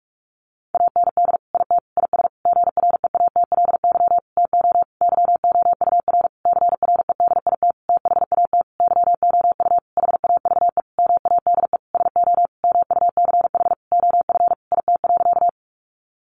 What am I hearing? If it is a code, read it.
Morse code, heard as a keyed tone, about 44 wpm, 700 Hz. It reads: AND IT IS GREATLY TO YOUR CREDIT THAT YOU HAVE MADE SO MUCH OF IT.
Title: Sentences from Top 1400 Words 45wpm